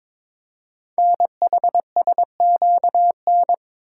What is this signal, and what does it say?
Morse code, heard as a keyed tone, about 22 wpm, 700 Hz. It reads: NHSQN
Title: Code Groups - 5 Letter:Numbers - 22wpm